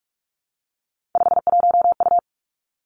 Morse code, heard as a keyed tone, about 46 wpm, 700 Hz. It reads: 51U
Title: Three Challenging Letter-Number Combinations 45wpm